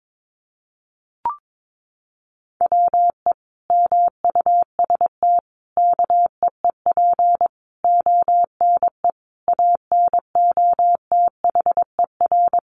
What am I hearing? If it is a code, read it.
Morse code, heard as a keyed tone, about 22 wpm, 700 Hz. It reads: WE MUST KEEP ONE ANOTHER
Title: Sentences from Top 600 Words 22wpm